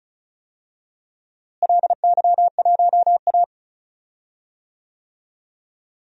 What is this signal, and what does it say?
Morse code, heard as a keyed tone, about 36 wpm, 700 Hz. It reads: LY1A
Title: DXCC-100 Call Signs - 4 Characters - Repeated 3x - 35wpm